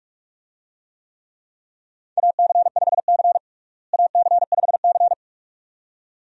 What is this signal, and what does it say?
Morse code, heard as a keyed tone, about 46 wpm, 700 Hz. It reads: AC5C AC5C
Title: USA Call Signs - 4 Characters - Repeated 2x - 45wpm